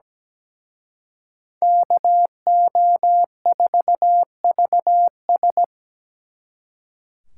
Morse code, heard as a keyed tone, about 20 wpm, 700 Hz. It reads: KO4VS
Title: Speed Racing - USA Call Signs 17wpm